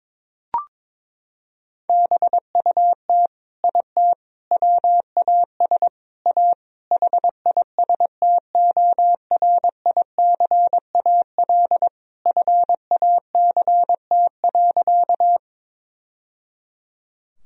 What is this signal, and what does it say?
Morse code, heard as a keyed tone, about 22 wpm, 700 Hz. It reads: BUT IT WAS A HISTORICAL FACT.
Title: Sentences from Top 1000 Words - Review Part 4 - 22wpm